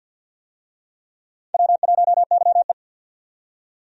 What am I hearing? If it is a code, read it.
Morse code, heard as a keyed tone, about 53 wpm, 700 Hz. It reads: W1YE
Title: USA Call Signs - 4 Characters - 50wpm